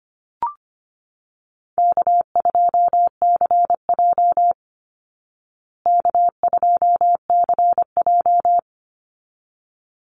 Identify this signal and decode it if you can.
Morse code, heard as a keyed tone, about 25 wpm, 700 Hz. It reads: K2CJ K2CJ
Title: Call Signs Repeated 3x - 25wpm